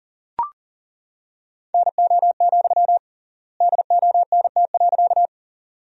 Morse code, heard as a keyed tone, about 40 wpm, 700 Hz. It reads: NO, DONT.
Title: 3x Word-Spacing - Sentences from Top 300 Words - 40wpm